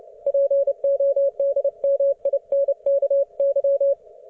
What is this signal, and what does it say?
Morse code, heard as a keyed tone, about 29 wpm, 560 Hz. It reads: PODMINKY